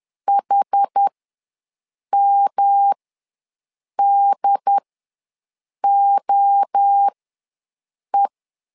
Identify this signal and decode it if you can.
Morse code, heard as a keyed tone, about 10 wpm, spaced as for 6 wpm, 800 Hz. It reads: HMDOE
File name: LEZIONI / 3 MWRO / 3 MWRO ws 10wpm 2310